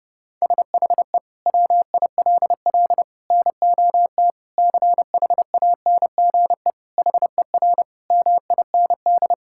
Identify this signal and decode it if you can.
Morse code, heard as a keyed tone, about 30 wpm, 700 Hz. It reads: SHE WILL NOT CHANGE HER MIND
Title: Sentences from Top 1000 Words - Review Part 6 - 30wpm